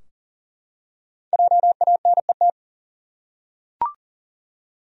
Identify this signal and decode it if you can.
Morse code, heard as a keyed tone, about 40 wpm, 700 Hz. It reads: JANET
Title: USA Common Names - 40wpm